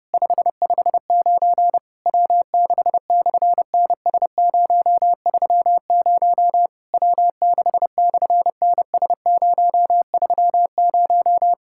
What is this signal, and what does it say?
Morse code, heard as a keyed tone, about 28 wpm, 700 Hz. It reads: 559 W6/NS030 W6/NS030
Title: SOTA QSO Practice 30wpm